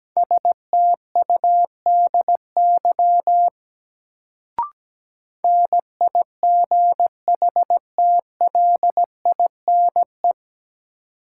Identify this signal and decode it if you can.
Morse code, heard as a keyed tone, about 17 wpm, 700 Hz. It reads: STUDY NIGHTLINE
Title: Sets of 5 Words - New Word Drill - 17wpm